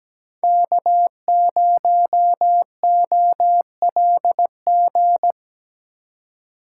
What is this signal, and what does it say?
Morse code, heard as a keyed tone, about 20 wpm, 700 Hz. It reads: K0OLG
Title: Speed Racing - USA 1x3 Call Signs 17wpm